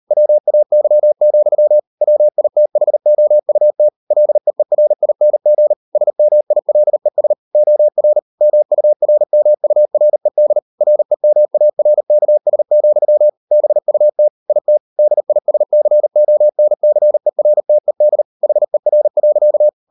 Morse code, heard as a keyed tone, about 38 wpm, 600 Hz. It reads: WAY, WITHOUT LEERING SMILES OR MURMURED REMARKS, BUT IT DISCONCERTED HER.